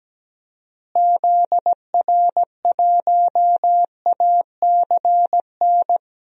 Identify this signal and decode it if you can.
Morse code, heard as a keyed tone, about 17 wpm, 700 Hz. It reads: ZR1ACN
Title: DXCC-100 Call Signs - 6 Characters - Repeated 3x - 17wpm